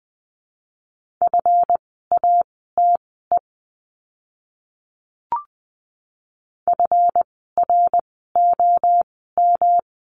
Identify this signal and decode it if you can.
Morse code, heard as a keyed tone, about 20 wpm, spaced as for 14 wpm, 700 Hz. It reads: FATE FROM
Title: 139 Lesson 23f - Letter f - Words 20wpm